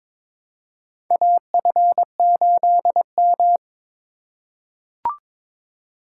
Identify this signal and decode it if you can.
Morse code, heard as a keyed tone, about 22 wpm, 700 Hz. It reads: AF8M
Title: USA Call Signs - 4 Characters - 4x Character Spacing - 22wpm